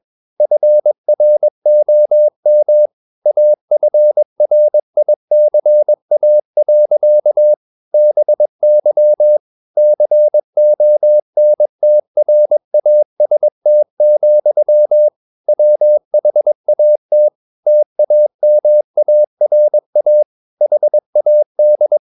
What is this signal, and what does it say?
Morse code, heard as a keyed tone, about 22 wpm, 600 Hz. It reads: FROM AFRICA. BY CONTRAST, WHAT TAMARA HAD
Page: CW med Gnister